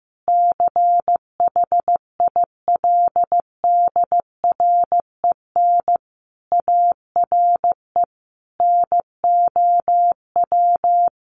Morse code, heard as a keyed tone, about 15 wpm, 700 Hz. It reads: CHILDREN ARE NOW
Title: Sentences from Top 1500 Words 15wpm